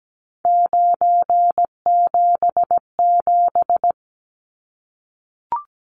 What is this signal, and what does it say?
Morse code, heard as a keyed tone, about 17 wpm, 700 Hz. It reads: 977
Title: Numbers - 3-Digit 17wpm